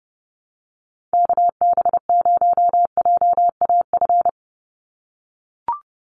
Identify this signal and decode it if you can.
Morse code, heard as a keyed tone, about 30 wpm, 700 Hz. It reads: KB0JAF
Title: USA Call Signs 30wpm